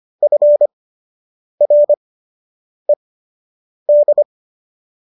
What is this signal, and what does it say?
Morse code, heard as a keyed tone, about 25 wpm, spaced as for 8 wpm, 590 Hz. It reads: FRED